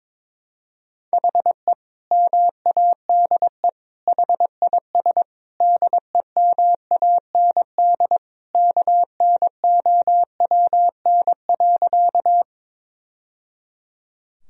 Morse code, heard as a keyed tone, about 21 wpm, 700 Hz. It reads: HE MADE HIS DEMAND KNOWN.
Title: Sentences from Top 1000 Words - Review Part 4 - 22wpm